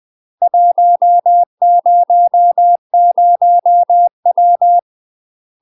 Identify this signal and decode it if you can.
Morse code, heard as a keyed tone, about 20 wpm, 700 Hz. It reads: 100W
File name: UK MASTER QSOs TRAINING / 20WPM / UK MASTER QSOs 0073